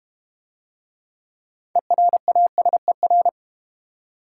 Morse code, heard as a keyed tone, about 32 wpm, 700 Hz. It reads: ERASER